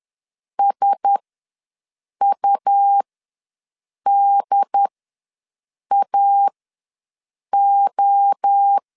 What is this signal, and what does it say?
Morse code, heard as a keyed tone, about 10 wpm, spaced as for 6 wpm, 800 Hz. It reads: SUDAO
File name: LEZIONI / 5 LBVC / 5 LBV ws 10wpm 3645